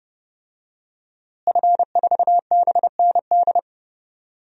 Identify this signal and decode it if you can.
Morse code, heard as a keyed tone, about 30 wpm, 700 Hz. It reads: F4BND
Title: DXCC-100 Call Signs - 5 Characters - Repeated 3x - 30wpm